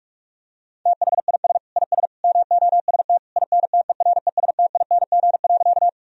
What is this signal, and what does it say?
Morse code, heard as a keyed tone, about 41 wpm, 700 Hz. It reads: THIS IS MOST INTERESTING.
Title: Sentences from Top 1000 Words - Review Part 1 - 45wpm